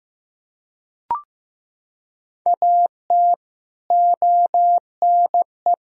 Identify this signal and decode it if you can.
Morse code, heard as a keyed tone, about 14 wpm, 700 Hz. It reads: AT ONE
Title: Sentences from Top 1000 Words - Review Part 1 - 15wpm